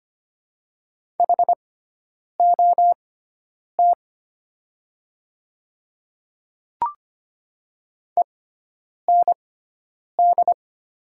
Morse code, heard as a keyed tone, about 25 wpm, spaced as for 9 wpm, 700 Hz. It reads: HOT END
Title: 6x Character Spacing - Top 200 Words - 25wpm